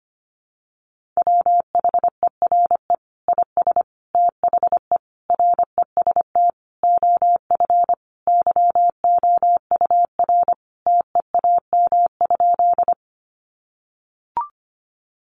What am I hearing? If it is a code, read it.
Morse code, heard as a keyed tone, about 25 wpm, 700 Hz. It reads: WHERE IS THE REST OF YOUR TEAM?
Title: Sentences from Top 1500 Words 25wpm